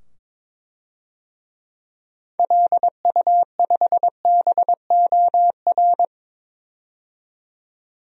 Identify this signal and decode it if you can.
Morse code, heard as a keyed tone, about 22 wpm, 700 Hz. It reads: LU5BOR
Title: DXCC-100 Call Signs - 6 Characters - 22wpm